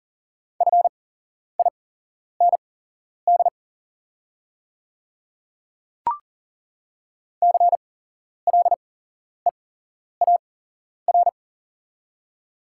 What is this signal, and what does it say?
Morse code, heard as a keyed tone, about 40 wpm, spaced as for 11 wpm, 700 Hz. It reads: FIND CLEAR
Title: 8x Character Spacing - Top 500 Words - 40wpm